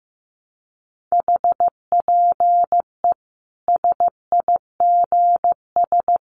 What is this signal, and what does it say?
Morse code, heard as a keyed tone, about 14 wpm, 700 Hz. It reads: HPE SIGS
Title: Common QSO Phrases 15wpm